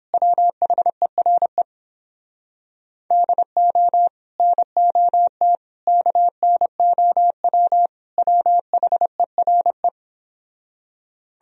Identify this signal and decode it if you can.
Morse code, heard as a keyed tone, about 30 wpm, 700 Hz. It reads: WHERE DO NOT KNOW WHERE